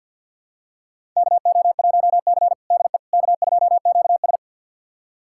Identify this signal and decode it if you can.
Morse code, heard as a keyed tone, about 49 wpm, 700 Hz. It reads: KO1C DE K2QS
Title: USA Callsign DE Callsign - 4 Characters - 50wpm